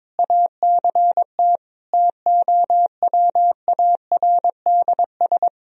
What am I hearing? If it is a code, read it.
Morse code, heard as a keyed tone, about 22 wpm, 700 Hz. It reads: ACT TOWARDS
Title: Sentences from Top 1000 Words - Review Part 4 - 22wpm